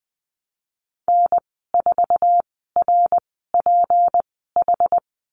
Morse code, heard as a keyed tone, about 19 wpm, spaced as for 14 wpm, 700 Hz. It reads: N4RPH
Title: 173 Lesson 28g - Number 9 - Call Signs 20wpm